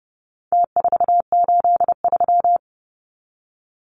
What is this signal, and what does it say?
Morse code, heard as a keyed tone, about 30 wpm, 700 Hz. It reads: T483
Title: Numbers - Contest Style - 4-Digit 30wpm